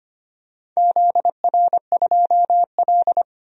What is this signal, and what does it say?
Morse code, heard as a keyed tone, about 25 wpm, 700 Hz. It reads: ZR2L
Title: DXCC-100 Call Signs - 4 Characters - Repeated 2x - 25wpm